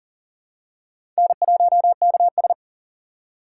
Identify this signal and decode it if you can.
Morse code, heard as a keyed tone, about 40 wpm, 700 Hz. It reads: N1KS
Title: USA Call Signs - 4 Characters - 3x Character Spacing - 40wpm